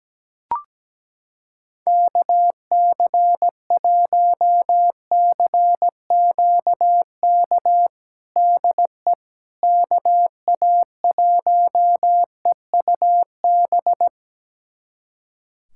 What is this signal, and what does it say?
Morse code, heard as a keyed tone, about 17 wpm, 700 Hz. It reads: KC1CQK DE KA1EUB
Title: USA Callsign DE Callsign 17wpm